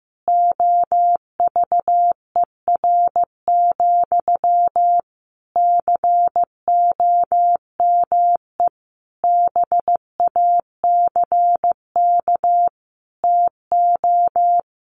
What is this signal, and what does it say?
Morse code, heard as a keyed tone, about 15 wpm, 700 Hz. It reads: OVER, COME BACK TO
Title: Sentences from Top 1200 Words 15wpm